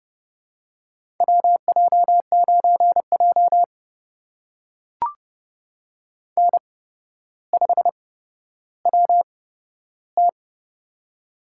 Morse code, heard as a keyed tone, about 30 wpm, spaced as for 13 wpm, 700 Hz. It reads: WJ9J N5WT
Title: USA Call Signs - 4 Characters - 8x Character Spacing - 30wpm